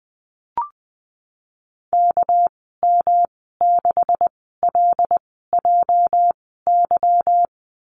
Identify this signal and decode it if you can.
Morse code, heard as a keyed tone, about 20 wpm, spaced as for 15 wpm, 700 Hz. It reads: KM6LJY
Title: 207 Lesson 33g - Letter j - Call Signs 20wpm